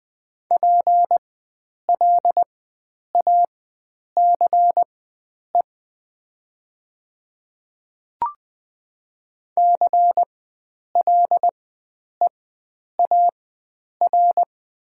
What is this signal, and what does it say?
Morse code, heard as a keyed tone, about 20 wpm, spaced as for 9 wpm, 700 Hz. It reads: PLACE CLEAR
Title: 4x Character Spacing - Top 500 Words - 20wpm